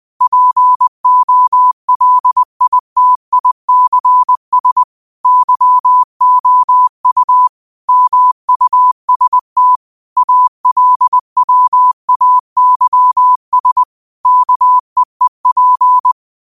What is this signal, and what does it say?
Morse code, heard as a keyed tone, about 20 wpm, 1000 Hz. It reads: POLITICS YOU MUST ALWAYS KEEP
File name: qotdfiles / 20wpm